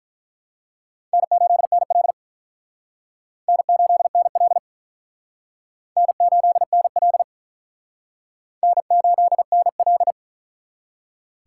Morse code, heard as a keyed tone, about 44 wpm, 700 Hz. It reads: N8NL N8NL N8NL N8NL